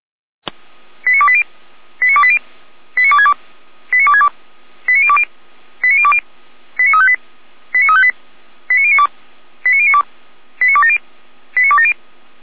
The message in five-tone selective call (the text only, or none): 89280 89280 89262 89262 89020 89020 89358 89358 89092 89092 89270 89270